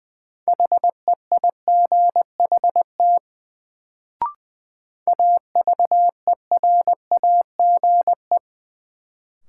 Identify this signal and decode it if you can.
Morse code, heard as a keyed tone, about 20 wpm, 700 Hz. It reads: HEIGHT AVERAGE
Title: Sets of 5 Words - New Word Drill - 20wpm